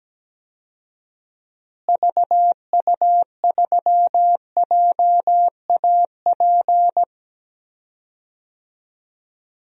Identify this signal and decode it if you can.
Morse code, heard as a keyed tone, about 17 wpm, 700 Hz. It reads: VU3JAP
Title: DXCC-100 Call Signs - 6 Characters - Repeated 3x - 17wpm